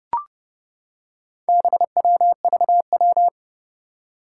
Morse code, heard as a keyed tone, about 30 wpm, 700 Hz. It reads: BWVW
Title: Four Challenging Letter-Number Combinations 30wpm